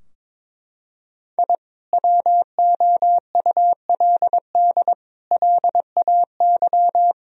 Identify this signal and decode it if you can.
Morse code, heard as a keyed tone, about 22 wpm, 700 Hz. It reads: I WOULD LAY
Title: Sentences from Top 1000 Words - Review Part 2 - 22wpm